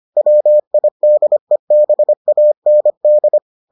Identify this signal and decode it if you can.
Morse code, heard as a keyed tone, about 25 wpm, 600 Hz. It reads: WIDEBAND